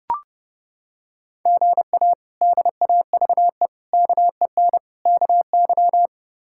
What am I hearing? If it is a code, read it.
Morse code, heard as a keyed tone, about 30 wpm, 700 Hz. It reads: GA DAVE KEN KY
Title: K1USN SST QSO Practice - 30wpm